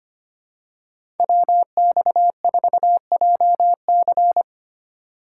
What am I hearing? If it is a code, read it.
Morse code, heard as a keyed tone, about 25 wpm, 700 Hz. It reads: WX4JC
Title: USA Call Signs - 5 Characters - 4x Character Spacing - 25wpm